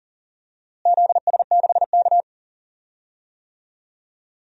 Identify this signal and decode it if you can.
Morse code, heard as a keyed tone, about 40 wpm, 700 Hz. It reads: ZS6K
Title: DXCC-100 Call Signs - 4 Characters - Repeated 2x - 40wpm